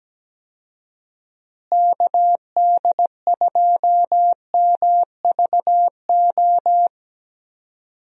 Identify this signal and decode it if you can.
Morse code, heard as a keyed tone, about 17 wpm, 700 Hz. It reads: KD2MVO
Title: USA Call Signs - 6 Characters - Repeated 2x - 17wpm